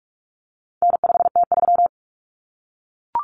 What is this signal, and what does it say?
Morse code, heard as a keyed tone, about 44 wpm, 700 Hz. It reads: N5T3
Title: Common Cut Numbers - 4-Digit 45wpm